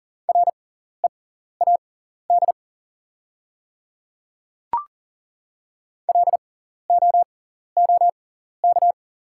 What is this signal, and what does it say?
Morse code, heard as a keyed tone, about 39 wpm, spaced as for 14 wpm, 700 Hz. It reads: READ LOOK
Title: 6x Character Spacing - Top 200 Words - 40wpm